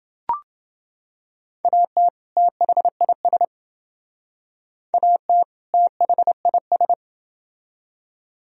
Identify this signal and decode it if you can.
Morse code, heard as a keyed tone, about 29 wpm, 700 Hz. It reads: AT THIS AT THIS